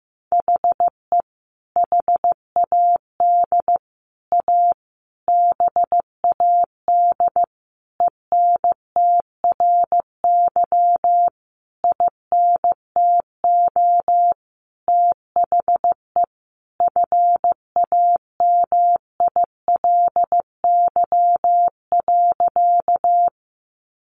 Morse code, heard as a keyed tone, about 15 wpm, 700 Hz. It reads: HE HAD A BAD ENTRY INTO THE FAMILY.
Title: Sentences from Top 1500 Words 15wpm